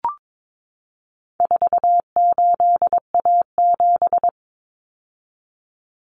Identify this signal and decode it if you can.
Morse code, heard as a keyed tone, about 22 wpm, 700 Hz. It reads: 48A7
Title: Common Cut Numbers - 4-Digit 22wpm